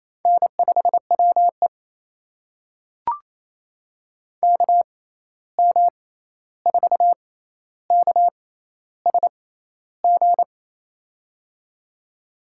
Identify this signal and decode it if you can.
Morse code, heard as a keyed tone, about 28 wpm, spaced as for 13 wpm, 700 Hz. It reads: N5WE KM4KSG
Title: USA Call Signs - 6x Character Spacing - 28wpm